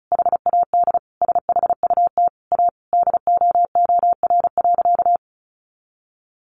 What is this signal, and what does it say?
Morse code, heard as a keyed tone, about 34 wpm, 700 Hz. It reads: HAD SHUT A DOOR.
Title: A Sound Of Thunder 35wpm